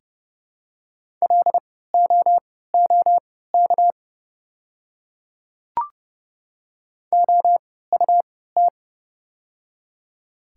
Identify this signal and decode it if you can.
Morse code, heard as a keyed tone, about 30 wpm, spaced as for 17 wpm, 700 Hz. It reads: LOOK OUT
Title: 3x Character Spacing - Top 200 Words - 30wpm